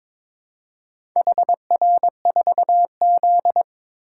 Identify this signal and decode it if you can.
Morse code, heard as a keyed tone, about 22 wpm, 700 Hz. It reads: HR4Z
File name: practice / 22wpm / Four Letter-Number Combinations - 22wpm